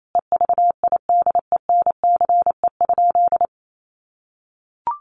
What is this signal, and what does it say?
Morse code, heard as a keyed tone, about 28 wpm, 700 Hz. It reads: EVIDENCE?
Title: Sentences from Top 1300 Words 28wpm